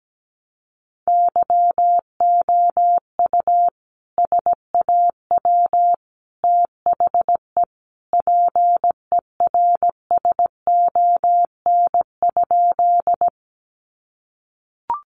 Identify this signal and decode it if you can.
Morse code, heard as a keyed tone, about 17 wpm, 700 Hz. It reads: YOU SAW THE PERSON?
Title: Sentences from Top 400 Words 17wpm